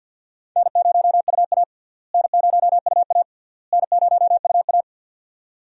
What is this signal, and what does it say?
Morse code, heard as a keyed tone, about 47 wpm, 700 Hz. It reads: N0UA N0UA N0UA
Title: USA Call Signs - 4 Characters - Repeated 3x - 50wpm